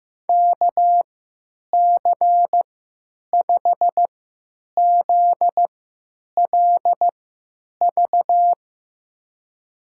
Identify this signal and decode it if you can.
Morse code, heard as a keyed tone, about 15 wpm, spaced as for 9 wpm, 700 Hz. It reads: KC5ZLV
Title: USA Call Signs - 6 Characters - 3x Character Spacing - 15wpm